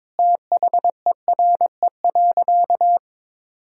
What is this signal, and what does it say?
Morse code, heard as a keyed tone, about 22 wpm, 700 Hz. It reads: THERE.
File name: practice / 22wpm / Sentences with 4-Word Phrases - 22wpm